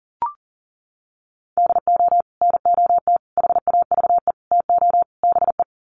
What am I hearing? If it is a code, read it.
Morse code, heard as a keyed tone, about 40 wpm, 700 Hz. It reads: DO NOT HAVE TO BE
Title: Sets of 5 Words 40wpm